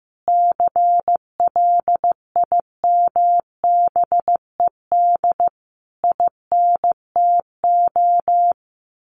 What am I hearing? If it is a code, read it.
Morse code, heard as a keyed tone, about 15 wpm, 700 Hz. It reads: CLIMBED INTO
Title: A Sound Of Thunder 15wpm